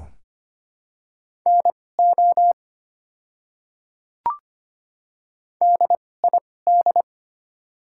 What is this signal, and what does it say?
Morse code, heard as a keyed tone, about 25 wpm, spaced as for 18 wpm, 700 Hz. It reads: NO DID
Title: 2x Character Spacing - Top 100 Words - 25wpm